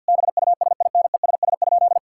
Morse code, heard as a keyed tone, about 52 wpm, 690 Hz. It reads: BUSINESS?
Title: Sentences from Top 1000 Words - Review Part 5 - 50wpm